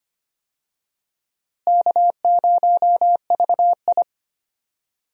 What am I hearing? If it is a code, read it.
Morse code, heard as a keyed tone, about 25 wpm, 700 Hz. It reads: K0VI
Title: USA Call Signs - 4 Characters - Repeated 3x - 25wpm